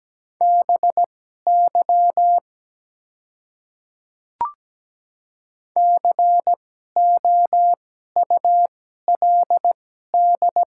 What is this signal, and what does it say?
Morse code, heard as a keyed tone, about 17 wpm, spaced as for 12 wpm, 700 Hz. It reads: BY COULD
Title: 2x Character Spacing - Top 200 Words - 17wpm